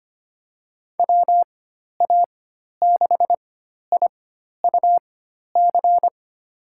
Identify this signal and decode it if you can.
Morse code, heard as a keyed tone, about 25 wpm, spaced as for 12 wpm, 700 Hz. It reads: WA6IUC
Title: USA Call Signs - 6 Characters - 4x Character Spacing - 25wpm